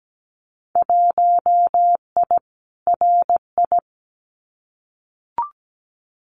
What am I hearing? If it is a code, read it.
Morse code, heard as a keyed tone, about 17 wpm, 700 Hz. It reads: 1I RI
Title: Winter Field Day Exchanges Encore 17wpm